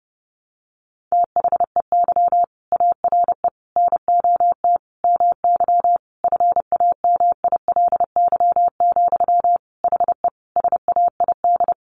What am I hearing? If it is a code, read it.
Morse code, heard as a keyed tone, about 28 wpm, 700 Hz. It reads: THEY ARE NOT MY FAMILY, HE SAID
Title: Sentences from Top 1100 Words 30wpm